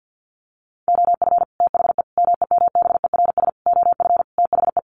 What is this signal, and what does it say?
Morse code, heard as a keyed tone, about 50 wpm, 700 Hz. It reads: OF THE MEMBERS OF THE
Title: Sets of 5 Words Encore2 50wpm